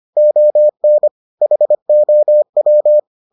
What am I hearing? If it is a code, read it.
Morse code, heard as a keyed tone, about 25 wpm, 600 Hz. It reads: ON HOW